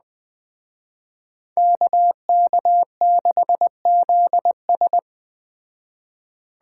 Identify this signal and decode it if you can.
Morse code, heard as a keyed tone, about 20 wpm, 700 Hz. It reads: KK6ZS
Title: Speed Racing - USA 2x2 Call Signs 17wpm